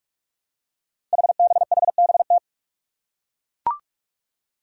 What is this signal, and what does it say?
Morse code, heard as a keyed tone, about 42 wpm, 700 Hz. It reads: HBHBT
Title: Five Challenging Letter-Number Combinations 45wpm